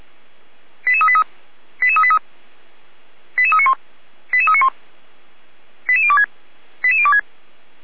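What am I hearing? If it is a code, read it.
Five-tone selective call (ZVEI), heard as quick runs of five tones, tones of about 70 ms. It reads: 88383 88383 88391 88391 88026 88026
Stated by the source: Five-tone ZVEI selective call 88391, received on SDR.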